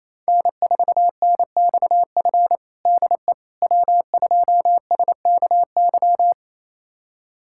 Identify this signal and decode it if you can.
Morse code, heard as a keyed tone, about 28 wpm, 700 Hz. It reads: N4NXF DE W2SKY
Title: USA Callsign DE Callsign - 5 Characters - 28wpm